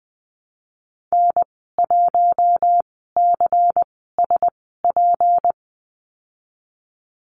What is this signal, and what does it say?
Morse code, heard as a keyed tone, about 19 wpm, spaced as for 14 wpm, 700 Hz. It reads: N1CSP
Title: 161 Lesson 26g - Letter g - Call Signs 20wpm